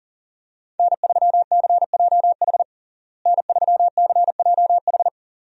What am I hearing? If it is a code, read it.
Morse code, heard as a keyed tone, about 40 wpm, 700 Hz. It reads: N3CJH N3CJH
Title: USA Call Signs - 5 Characters - Repeated 2x - 40wpm